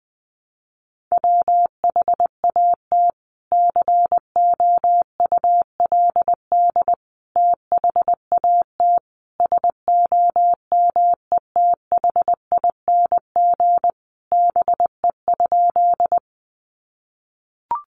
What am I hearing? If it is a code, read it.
Morse code, heard as a keyed tone, about 20 wpm, 700 Hz. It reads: WHAT COULD THAT SOMETHING BE?
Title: Sentences from Top 600 Words 20wpm